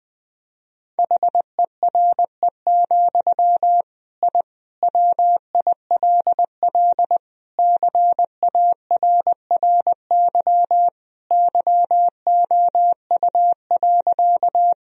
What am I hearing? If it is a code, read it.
Morse code, heard as a keyed tone, about 20 wpm, 700 Hz. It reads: HERE, I WILL CARRY YOU.
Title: Sentences from Top 1000 Words - Review Part 1 - 20wpm